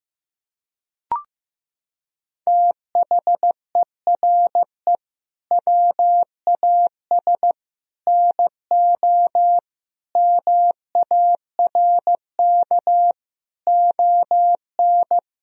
Morse code, heard as a keyed tone, about 15 wpm, 700 Hz. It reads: THERE WAS NO MARK ON
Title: Sentences from Top 1000 Words - Review Part 1 - 15wpm